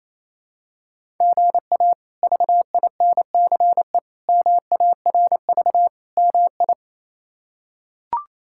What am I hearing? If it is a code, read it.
Morse code, heard as a keyed tone, about 28 wpm, 700 Hz. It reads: GA VINCE MARV MI
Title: K1USN SST QSO Practice - 28wpm